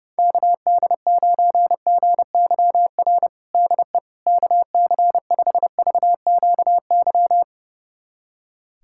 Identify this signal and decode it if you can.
Morse code, heard as a keyed tone, about 30 wpm, 700 Hz. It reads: KD9GYR DE KC5VQY
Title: USA Callsign DE Callsign - 6 Characters - 30wpm